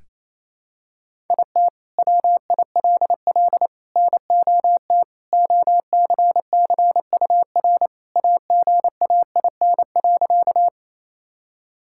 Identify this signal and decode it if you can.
Morse code, heard as a keyed tone, about 28 wpm, 700 Hz. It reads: IT WILL NOT OCCUR AGAIN.
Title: Sentences from Top 1000 Words - Review Part 6 - 28wpm